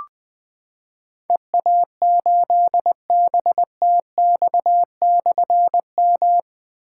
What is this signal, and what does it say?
Morse code, heard as a keyed tone, about 20 wpm, 700 Hz. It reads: EA8BTX/M
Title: Difficult Call Signs 20wpm